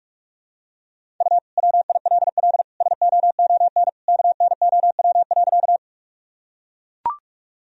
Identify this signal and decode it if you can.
Morse code, heard as a keyed tone, about 43 wpm, 700 Hz. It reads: U WILL SOON KNOW.
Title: Sentences from Top 400 Words and Abbreviations 45wpm